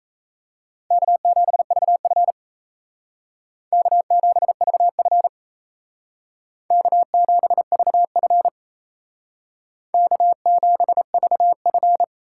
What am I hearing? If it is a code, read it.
Morse code, heard as a keyed tone, about 37 wpm, 700 Hz. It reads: K7VF K7VF K7VF K7VF